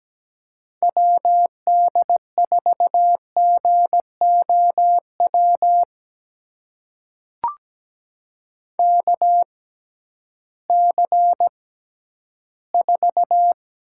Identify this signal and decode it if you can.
Morse code, heard as a keyed tone, about 17 wpm, spaced as for 11 wpm, 700 Hz. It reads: WD4GOW KC4
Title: USA Call Signs - 6 Characters - 6x Character Spacing - 17wpm